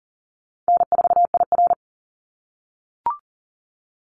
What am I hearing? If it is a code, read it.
Morse code, heard as a keyed tone, about 38 wpm, 700 Hz. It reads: N4IR
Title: Easy Call Signs - 40wpm